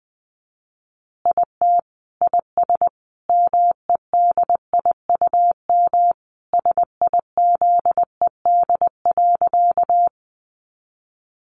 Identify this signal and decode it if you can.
Morse code, heard as a keyed tone, about 19 wpm, 700 Hz. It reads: IT IS MEDIUM SIZED.
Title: Easy Phrases 20wpm